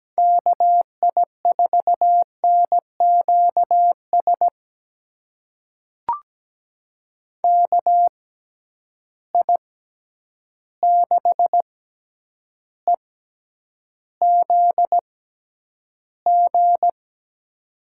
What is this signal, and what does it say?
Morse code, heard as a keyed tone, about 17 wpm, spaced as for 9 wpm, 700 Hz. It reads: KI4NQS KI6EZG
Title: USA Call Signs - 6 Characters - 6x Character Spacing - 17wpm